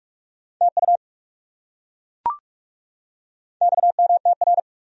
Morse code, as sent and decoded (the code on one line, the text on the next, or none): - ..- / -..- -- - .-.
TU XMTR